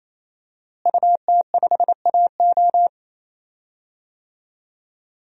Morse code, heard as a keyed tone, about 28 wpm, 700 Hz. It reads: UT5AO
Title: DXCC-100 Call Signs - 5 Characters - Repeated 3x - 28wpm